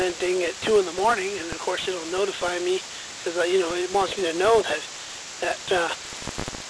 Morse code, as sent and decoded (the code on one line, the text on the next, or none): none